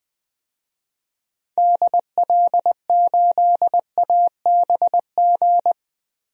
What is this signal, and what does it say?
Morse code, heard as a keyed tone, about 20 wpm, 700 Hz. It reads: DL8ABG
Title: DXCC-100 Call Signs - 6 Characters - 20wpm